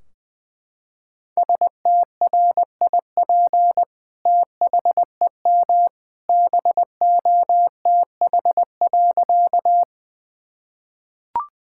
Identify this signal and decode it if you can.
Morse code, heard as a keyed tone, about 20 wpm, 700 Hz. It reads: STRIP THEM BOTH.
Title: Sentences from Top 1600 Words 20wpm